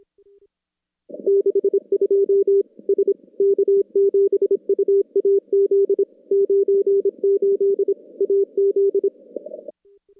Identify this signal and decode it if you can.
Morse code, heard as a keyed tone, about 25 wpm, 410 Hz. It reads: 62 S K7UAZ 98 AZ
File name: contests / CW-SS-k5tr- -2300